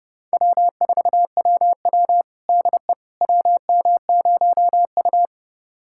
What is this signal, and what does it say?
Morse code, heard as a keyed tone, about 32 wpm, 700 Hz. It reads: W4WW DE WM0U
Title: USA Callsign DE Callsign - 4 Characters - 30wpm